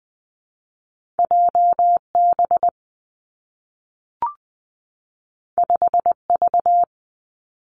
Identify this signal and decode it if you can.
Morse code, heard as a keyed tone, about 19 wpm, 700 Hz. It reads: J B 5 V
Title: Two Challenging Letter-Number Combinations 20wpm